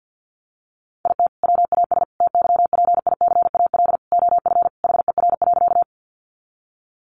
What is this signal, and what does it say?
Morse code, heard as a keyed tone, about 50 wpm, 700 Hz. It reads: IT WAS TYPICAL OF HER.
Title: Sentences from Top 1100 Words 50wpm